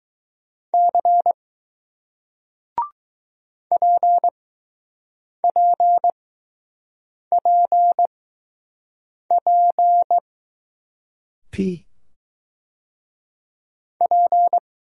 Morse code, heard as a keyed tone, about 20 wpm, spaced as for 6 wpm, 700 Hz. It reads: C PPPP P